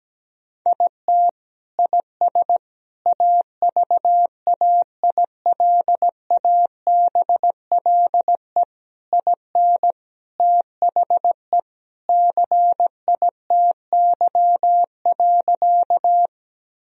Morse code, heard as a keyed tone, about 17 wpm, 700 Hz. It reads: IT IS AVAILABLE IN THE CITY.
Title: Sentences from Top 1000 Words - Review Part 3 - 17wpm